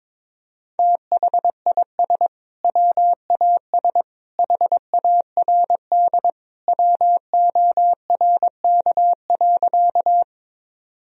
Morse code, heard as a keyed tone, about 22 wpm, 700 Hz. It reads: THIS WAS HARD WORK.
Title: Sentences from Top 1000 Words - Review Part 1 - 22wpm